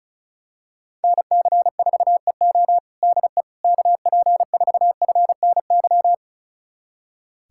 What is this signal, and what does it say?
Morse code, heard as a keyed tone, about 36 wpm, 700 Hz. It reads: NC4EO DE KP4FNY
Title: USA Callsign DE Callsign 35wpm